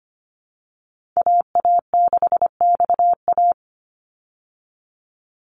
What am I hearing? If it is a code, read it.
Morse code, heard as a keyed tone, about 25 wpm, 700 Hz. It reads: AA6XA
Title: Call Signs - 25wpm